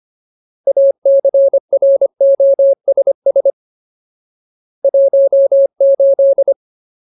Morse code, heard as a keyed tone, about 25 wpm, 550 Hz. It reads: ACROSS 18
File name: Ultimate Word List / 25WPM Ultimate